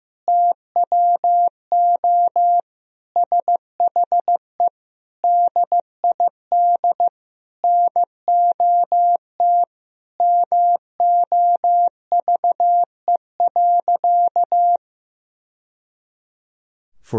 Morse code, heard as a keyed tone, about 16 wpm, 700 Hz. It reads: TWO SHE DID NOT MOVE.
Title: Sentences from Top 1000 Words - Review Part 1 - 15wpm